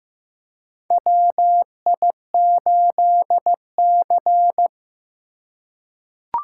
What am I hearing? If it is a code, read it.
Morse code, heard as a keyed tone, about 15 wpm, 700 Hz. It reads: WI8C
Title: USA Call Signs - 4 Characters - 3x Character Spacing - 15wpm